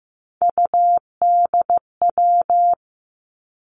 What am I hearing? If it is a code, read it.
Morse code, heard as a keyed tone, about 15 wpm, 700 Hz. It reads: UDW
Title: Three Challenging Letter-Number Combinations 15wpm